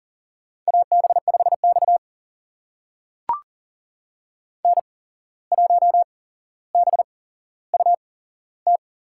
Morse code, heard as a keyed tone, about 40 wpm, spaced as for 16 wpm, 700 Hz. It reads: AB5X N1BUT
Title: USA Call Signs - 8x Character Spacing - 40wpm